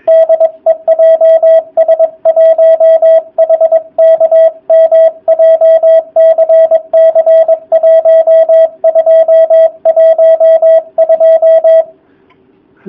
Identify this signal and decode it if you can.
Morse code, heard as a keyed tone, about 22 wpm, 660 Hz, with radio noise. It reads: DE JS1HKM JCC 1212
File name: jh8cbh / 2019cw / sp23- w